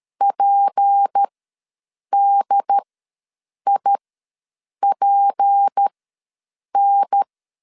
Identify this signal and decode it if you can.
Morse code, heard as a keyed tone, about 13 wpm, spaced as for 7 wpm, 800 Hz. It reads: PDIPN
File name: LEZIONI / 6 PKX0 / 6 P ws 12wpm 3117